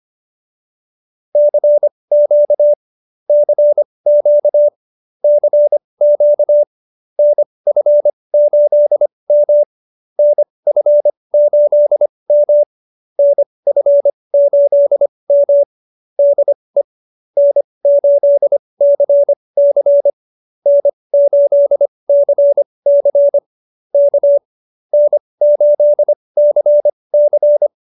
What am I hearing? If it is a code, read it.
Morse code, heard as a keyed tone, about 25 wpm, spaced as for 20 wpm, 590 Hz. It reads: CQ CQ CQ NF8M NF8M NF8M DE N8CC N8CC K N8CC